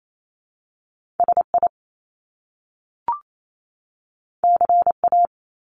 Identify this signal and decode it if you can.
Morse code, heard as a keyed tone, about 28 wpm, 700 Hz. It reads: SI CA
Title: Top Two Letter Combinations in Words 28wpm